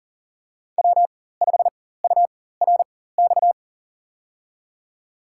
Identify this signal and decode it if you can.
Morse code, heard as a keyed tone, about 40 wpm, spaced as for 19 wpm, 700 Hz. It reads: W5URX
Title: USA Call Signs - 5 Characters - 4x Character Spacing - 40wpm